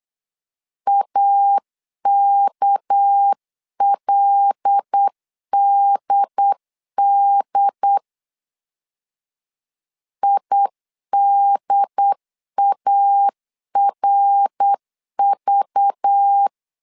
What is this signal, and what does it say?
Morse code, heard as a keyed tone, about 8 wpm, 800 Hz. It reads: AKLDD IDARV
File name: CW / DET / 1 6 1500C S C 8wpm 5219